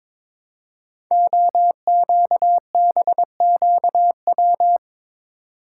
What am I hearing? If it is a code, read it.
Morse code, heard as a keyed tone, about 23 wpm, 700 Hz. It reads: OQBQW
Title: Code Groups - 5 Letters - 22wpm